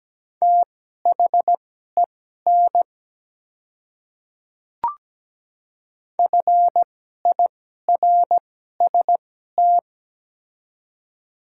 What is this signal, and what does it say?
Morse code, heard as a keyed tone, about 17 wpm, spaced as for 12 wpm, 700 Hz. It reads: THEN FIRST
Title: 2x Character Spacing - Top 100 Words - 17wpm